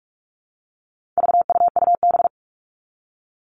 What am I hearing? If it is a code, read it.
Morse code, heard as a keyed tone, about 44 wpm, 700 Hz. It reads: VUUB
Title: All Cut Numbers - 4-Digit 45wpm